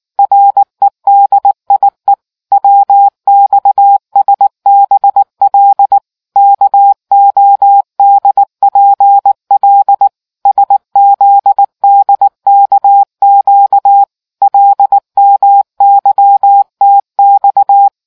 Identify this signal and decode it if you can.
Morse code, heard as a keyed tone, about 19 wpm, 800 Hz. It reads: REDIE WXSBL KODPL SZDKQ LMYTX